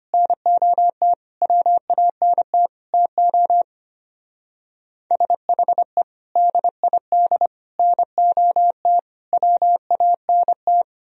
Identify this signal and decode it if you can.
Morse code, heard as a keyed tone, about 32 wpm, 700 Hz. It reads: NOT WANT TO SHE DID NOT WANT